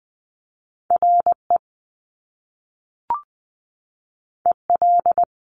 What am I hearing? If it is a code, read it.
Morse code, heard as a keyed tone, about 20 wpm, 700 Hz. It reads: RE EL
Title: Top Two Letter Combinations in Words 20wpm